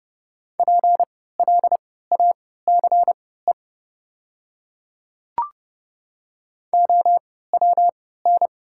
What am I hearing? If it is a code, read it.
Morse code, heard as a keyed tone, about 30 wpm, spaced as for 17 wpm, 690 Hz. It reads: PLACE OWN
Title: 3x Character Spacing - Top 300 Words - 30wpm